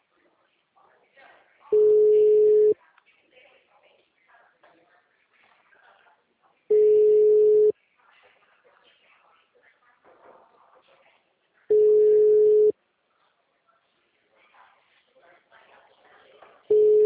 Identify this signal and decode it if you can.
Ringback tone (Europe, 425 Hz), with speech.